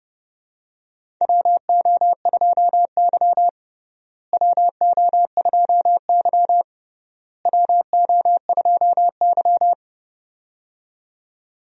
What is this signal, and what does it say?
Morse code, heard as a keyed tone, about 29 wpm, 700 Hz. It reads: WO2Y WO2Y WO2Y
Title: USA Call Signs - 4 Characters - Repeated 3x - 30wpm